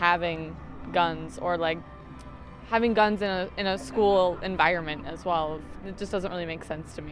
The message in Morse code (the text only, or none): none